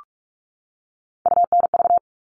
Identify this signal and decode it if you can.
Morse code, heard as a keyed tone, about 44 wpm, 700 Hz. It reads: UNV
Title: All Cut Numbers - 3-Digit 45wpm